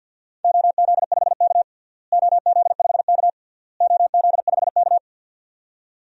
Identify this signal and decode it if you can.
Morse code, heard as a keyed tone, about 52 wpm, 700 Hz. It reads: OZ5K OZ5K OZ5K
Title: DXCC-100 Call Signs - 4 Characters - Repeated 3x - 50wpm